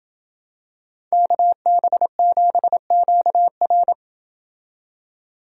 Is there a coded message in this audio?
Yes: Morse code from 1.1 to 3.9 s, heard as a keyed tone, about 27 wpm, 700 Hz. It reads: KB7QR